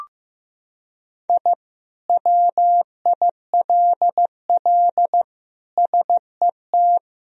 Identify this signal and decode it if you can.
Morse code, heard as a keyed tone, about 15 wpm, 700 Hz. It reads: I WILL SET
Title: Sentences from Top 1000 Words - Review Part 2 - 15wpm